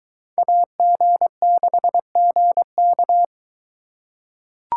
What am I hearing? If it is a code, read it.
Morse code, heard as a keyed tone, about 22 wpm, 700 Hz. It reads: AG6GK